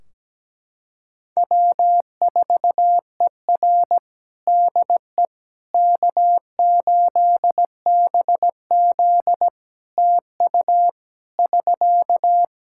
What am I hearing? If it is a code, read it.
Morse code, heard as a keyed tone, about 17 wpm, 700 Hz. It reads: W4ER DE K8BZ TU <SK>
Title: Practice QSO - Exchange 4 - Ending - 17wpm